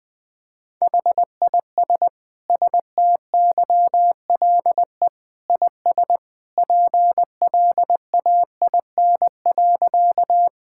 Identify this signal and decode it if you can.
Morse code, heard as a keyed tone, about 20 wpm, 700 Hz. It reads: HIS STYLE IS PLAIN.
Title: Sentences from Top 1000 Words - Review Part 5 - 20wpm